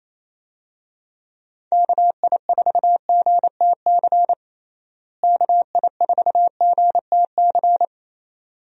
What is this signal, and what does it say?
Morse code, heard as a keyed tone, about 28 wpm, 700 Hz. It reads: KI4GTC KI4GTC
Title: USA Call Signs - 6 Characters - Repeated 2x - 28wpm